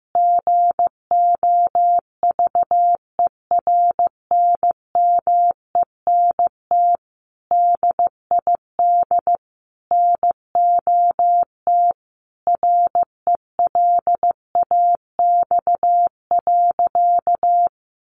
Morse code, heard as a keyed tone, about 15 wpm, 700 Hz. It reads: GOVERNMENT DID NOT RELAX.
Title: Sentences from Top 1400 Words 15wpm